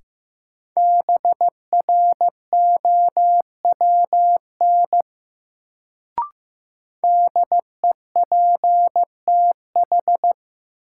Morse code, heard as a keyed tone, about 15 wpm, 700 Hz. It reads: BROWN DEPTH
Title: Sentences from Top 1200 Words - New Word Drill - 15wpm